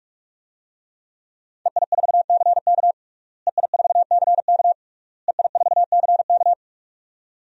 Morse code, heard as a keyed tone, about 41 wpm, 700 Hz. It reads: EI4CK EI4CK EI4CK
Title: DXCC-100 Call Signs - 5 Characters - Repeated 3x - 45wpm